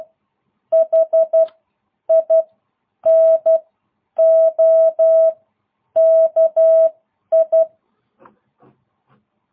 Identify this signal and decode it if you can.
Morse code, heard as a keyed tone, about 12 wpm, spaced as for 9 wpm, 660 Hz. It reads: HINOKI